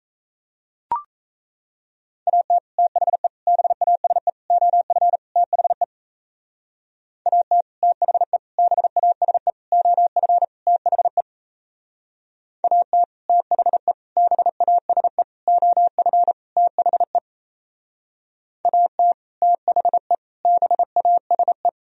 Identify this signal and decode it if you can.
Morse code, heard as a keyed tone, about 37 wpm, 700 Hz. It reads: AT THE BASE OF THE AT THE BASE OF THE AT THE BASE OF THE AT THE BASE